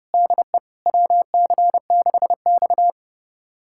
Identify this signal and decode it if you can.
Morse code, heard as a keyed tone, about 30 wpm, 700 Hz. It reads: DE WC6X
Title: USA Callsign DE Callsign - 4 Characters - 30wpm